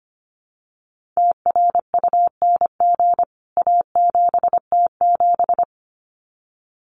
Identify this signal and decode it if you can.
Morse code, heard as a keyed tone, about 25 wpm, 700 Hz. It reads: TRUNG A7T7
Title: CWT Exchange 25wpm